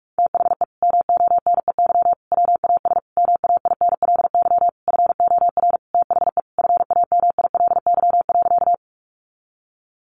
Morse code, heard as a keyed tone, about 44 wpm, 700 Hz. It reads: THE MONEY WAS MAINLY FOR THE FAMILY.
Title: Sentences from Top 1400 Words 45wpm